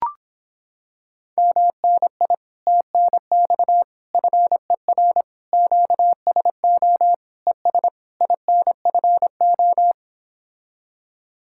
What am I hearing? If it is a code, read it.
Morse code, heard as a keyed tone, about 26 wpm, 700 Hz. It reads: MNI TNX FER QSO ES INFO